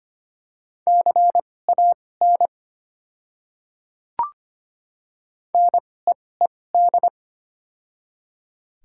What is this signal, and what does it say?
Morse code, heard as a keyed tone, about 25 wpm, spaced as for 18 wpm, 690 Hz. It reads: CAN NEED
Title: 2x Character Spacing - Top 200 Words - 25wpm